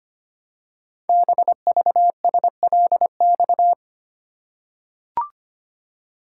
Morse code, heard as a keyed tone, about 25 wpm, 700 Hz. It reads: BVSLX
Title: Code Groups - 5 Letter:Numbers - 25wpm